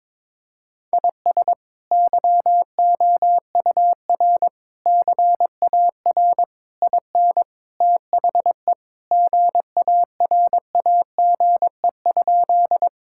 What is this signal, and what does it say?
Morse code, heard as a keyed tone, about 21 wpm, 700 Hz. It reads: IS YOUR CAR IN THE GARAGE?
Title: Sentences from Top 1600 Words 22wpm